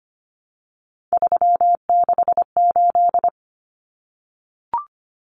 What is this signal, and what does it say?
Morse code, heard as a keyed tone, about 25 wpm, 700 Hz. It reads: 368
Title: Numbers - 3-Digit 25wpm